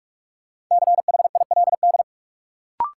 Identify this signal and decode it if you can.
Morse code, heard as a keyed tone, about 46 wpm, 700 Hz. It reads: CHILD
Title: Sentences from Top 800 Words - New Word Drill - 45wpm